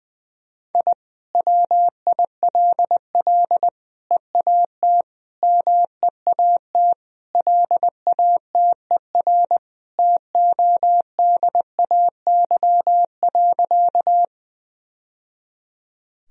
Morse code, heard as a keyed tone, about 20 wpm, 700 Hz. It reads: I WILL EAT MEAT LATER TODAY.
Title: Sentences from Top 1000 Words - Review Part 3 - 20wpm